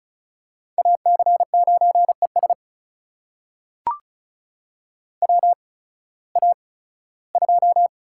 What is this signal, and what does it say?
Morse code, heard as a keyed tone, about 36 wpm, spaced as for 19 wpm, 700 Hz. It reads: AC9ES WA2
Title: USA Call Signs - 8x Character Spacing - 35wpm